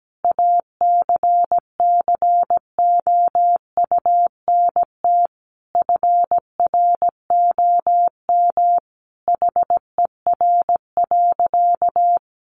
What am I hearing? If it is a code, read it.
Morse code, heard as a keyed tone, about 17 wpm, 700 Hz. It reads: ACCOUNT FROM HER.
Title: Sentences from Top 900 Words 17wpm